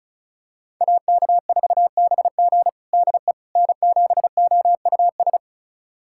Morse code, heard as a keyed tone, about 36 wpm, 700 Hz. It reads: AK4BG DE N7OUS
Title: USA Callsign DE Callsign - 5 Characters - 35wpm